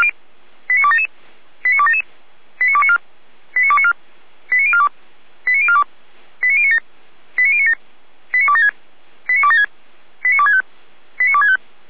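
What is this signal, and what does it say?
Five-tone selective call (ZVEI), heard as quick runs of five tones, tones of about 70 ms. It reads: 89277 89277 89294 89294 89042 89042 89097 89097 89276 89276 89265 89265
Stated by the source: ZVEI five-tone call 89310 received on SDR.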